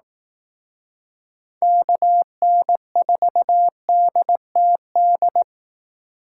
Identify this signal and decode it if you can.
Morse code, heard as a keyed tone, about 18 wpm, 700 Hz. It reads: KN4DTD
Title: Speed Racing - USA 2x3 Call Signs 15wpm